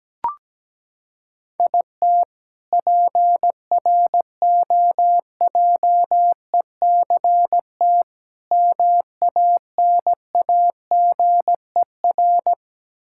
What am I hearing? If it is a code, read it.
Morse code, heard as a keyed tone, about 17 wpm, 700 Hz. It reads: IT PROJECT MANAGER
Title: K1USN SST QSO Practice 17wpm